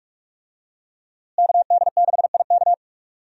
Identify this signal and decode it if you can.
Morse code, heard as a keyed tone, about 44 wpm, 700 Hz. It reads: KD6IK
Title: USA Call Signs - 5 Characters - Repeated 2x - 45wpm